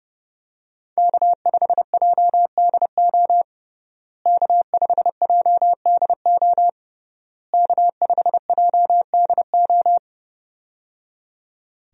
Morse code, heard as a keyed tone, about 28 wpm, 700 Hz. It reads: K5JDO K5JDO K5JDO
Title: USA Call Signs - 5 Characters - Repeated 3x - 30wpm